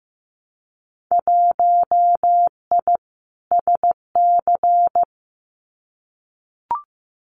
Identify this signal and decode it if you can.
Morse code, heard as a keyed tone, about 15 wpm, 700 Hz. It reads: 1I SC
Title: Winter Field Day Exchanges 15wpm